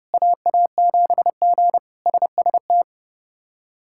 Morse code, heard as a keyed tone, about 30 wpm, 700 Hz. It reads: AA7G SST
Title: K1USN SST QSO Practice - 30wpm